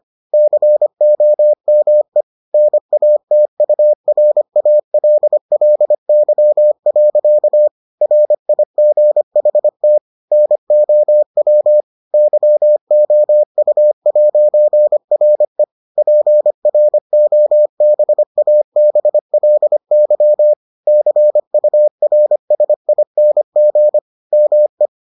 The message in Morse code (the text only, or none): COME NATURALLY. RIGHT NOW YOU'RE PROBABLY CURSING ME